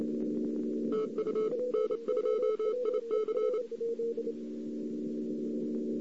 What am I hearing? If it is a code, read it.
Morse code, heard as a keyed tone, about 28 wpm, 1300 Hz. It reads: TU N2IC
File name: CW / 2008-ss-cw-k5tr-op-n5rz -02:00